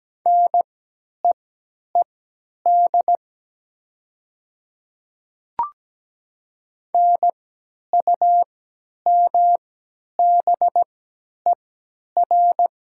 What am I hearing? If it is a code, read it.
Morse code, heard as a keyed tone, about 17 wpm, spaced as for 10 wpm, 700 Hz. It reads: NEED NUMBER
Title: 3x Character Spacing - Top 200 Words - 17wpm